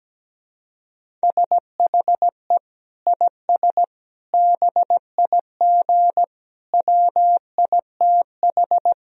Morse code, heard as a keyed tone, about 17 wpm, 700 Hz. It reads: SHE IS BIG WITH
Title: Sentences from Top 1000 Words - Review Part 1 - 17wpm